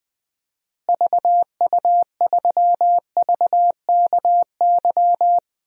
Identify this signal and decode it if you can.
Morse code, heard as a keyed tone, about 20 wpm, 700 Hz. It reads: VU3VKY
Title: DXCC-100 Call Signs - 6 Characters - Repeated 3x - 20wpm